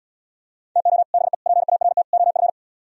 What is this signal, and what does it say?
Morse code, heard as a keyed tone, about 78 wpm, 700 Hz. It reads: TO BE PART OF